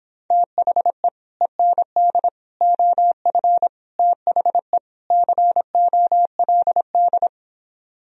Morse code, heard as a keyed tone, about 26 wpm, 700 Hz. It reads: THE END OF THE COLD